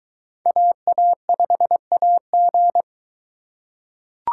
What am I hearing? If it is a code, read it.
Morse code, heard as a keyed tone, about 22 wpm, 700 Hz. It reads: AA5AG